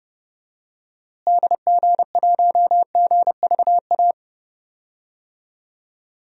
Morse code, heard as a keyed tone, about 30 wpm, 700 Hz. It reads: DG1GVA
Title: DXCC-100 Call Signs - 6 Characters - Repeated 2x - 30wpm